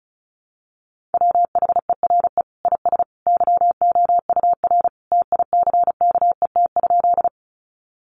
Morse code, heard as a keyed tone, about 34 wpm, 700 Hz. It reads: WHERE IS YOUR TICKET?
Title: Sentences from Top 1400 Words 35wpm